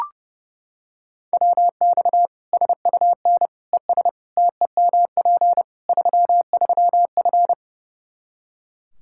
Morse code, heard as a keyed tone, about 30 wpm, 700 Hz. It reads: WX SUN ES TEMP 33F
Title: K1USN SST QSO Practice 30wpm